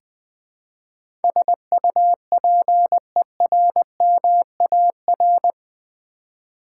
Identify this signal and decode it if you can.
Morse code, heard as a keyed tone, about 20 wpm, 700 Hz. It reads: SUPERMAR
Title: Build A Word - 4 Letters At A Time - 11-Letter Words - 20wpm